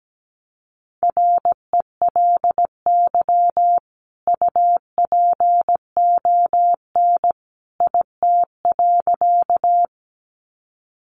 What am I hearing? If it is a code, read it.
Morse code, heard as a keyed tone, about 17 wpm, 700 Hz. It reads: RELY UPON IT.
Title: Sentences from Top 1200 Words 17wpm